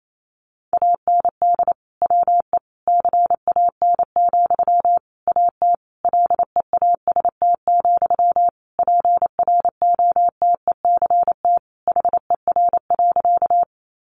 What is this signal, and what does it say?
Morse code, heard as a keyed tone, about 28 wpm, 700 Hz. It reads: AND WE CAN, AT LEAST, PROTECT HER.
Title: Sentences from Top 900 Words 28wpm